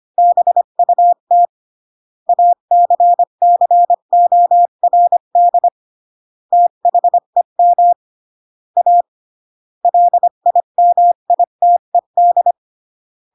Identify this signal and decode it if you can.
Morse code, heard as a keyed tone, about 25 wpm, 700 Hz. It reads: BUT ACCORD THEM A LIMITED